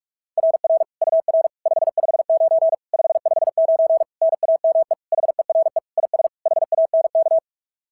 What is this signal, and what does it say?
Morse code, heard as a keyed tone, about 44 wpm, 660 Hz. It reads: RR UR 559 559 NAME HERE IS HANK
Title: SOTA QSO Practice 45wpm